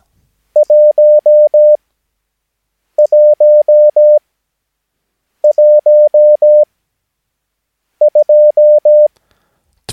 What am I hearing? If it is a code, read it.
Morse code, heard as a keyed tone, about 17 wpm, spaced as for 6 wpm, 600 Hz. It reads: 1112
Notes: Övningsmaterial med alfabetet A till Ö samt siffror 0 till 9 enligt International/Continental Morse Code.